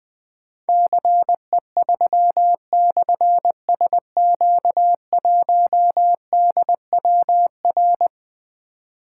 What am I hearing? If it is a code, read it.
Morse code, heard as a keyed tone, about 20 wpm, 700 Hz. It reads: CE3/SQ1DWR
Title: Difficult Call Signs 20wpm